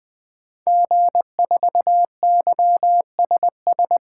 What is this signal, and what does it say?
Morse code, heard as a keyed tone, about 20 wpm, 700 Hz. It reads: G4YSS
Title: 2023 SOTA Activator Call Signs - 20wpm